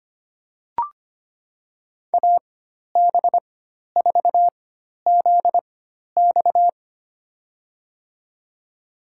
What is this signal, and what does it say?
Morse code, heard as a keyed tone, about 25 wpm, spaced as for 12 wpm, 700 Hz. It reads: AB4ZX
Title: USA Call Signs - 5 Characters - 4x Character Spacing - 25wpm